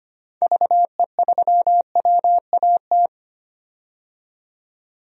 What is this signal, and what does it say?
Morse code, heard as a keyed tone, about 25 wpm, 700 Hz. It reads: VE3WAT
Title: DXCC-100 Call Signs - 6 Characters - 25wpm